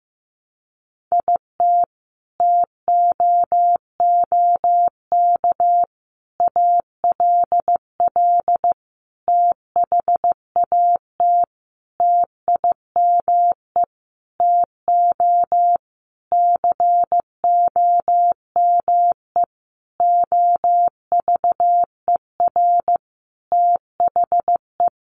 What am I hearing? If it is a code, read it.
Morse code, heard as a keyed tone, about 15 wpm, 700 Hz. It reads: IT TOOK ALL THAT TIME TO COME OVER THE
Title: Sentences from Top 300 Words 15wpm